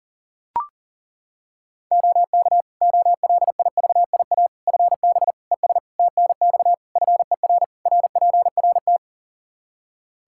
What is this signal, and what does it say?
Morse code, heard as a keyed tone, about 40 wpm, 700 Hz. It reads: OK OLIVIA FB ES TNX FER RPRT
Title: Practice QSO - Exchange 2 - Info - 40wpm